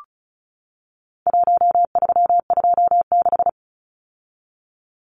Morse code, heard as a keyed tone, about 34 wpm, 700 Hz. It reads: 1326
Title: Numbers - Contest Style - 4-Digit 35wpm